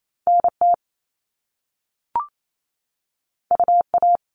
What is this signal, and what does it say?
Morse code, heard as a keyed tone, about 28 wpm, 700 Hz. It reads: NT UA
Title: Top Two Letter Combinations in Words 28wpm